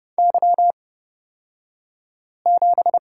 Morse code, heard as a keyed tone, about 30 wpm, spaced as for 5 wpm, 700 Hz. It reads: Y7
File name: Lesson 37d - Letter x - Two Characters 30wpm